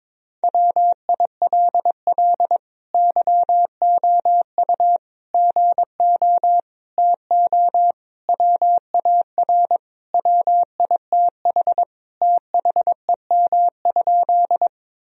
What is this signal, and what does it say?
Morse code, heard as a keyed tone, about 22 wpm, 700 Hz. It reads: WILL YOU GO TO WAR WITH THEM?
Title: Sentences from Top 1000 Words - Review Part 2 - 22wpm